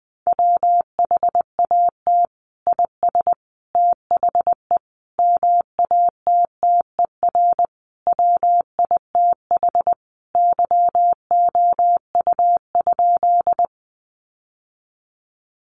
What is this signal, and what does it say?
Morse code, heard as a keyed tone, about 20 wpm, 700 Hz. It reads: WHAT IS THE MATTER WITH YOU?
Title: Sentences from Top 700 Words 20wpm